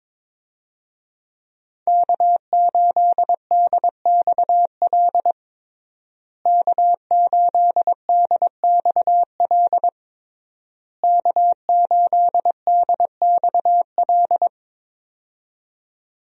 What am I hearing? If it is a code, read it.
Morse code, heard as a keyed tone, about 22 wpm, 700 Hz. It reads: K8DXL K8DXL K8DXL
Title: USA Call Signs - 5 Characters - Repeated 3x - 22wpm